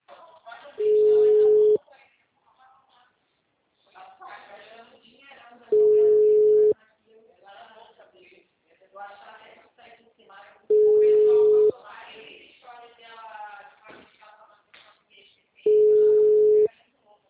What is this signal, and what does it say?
Ringback tone (Europe, 425 Hz), with speech.
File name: out- -2014- - - .11157